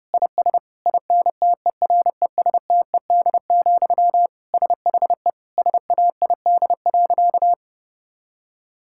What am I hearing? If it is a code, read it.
Morse code, heard as a keyed tone, about 30 wpm, 700 Hz. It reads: IS INTERESTED, SHE SAID.
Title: Sentences from Top 1000 Words - Review Part 6 - 30wpm